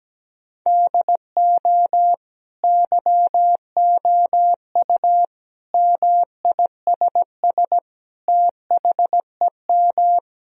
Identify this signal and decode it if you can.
Morse code, heard as a keyed tone, about 17 wpm, 700 Hz. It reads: DO YOU MISS THEM
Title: Sentences from Top 1000 Words - Review Part 3 - 17wpm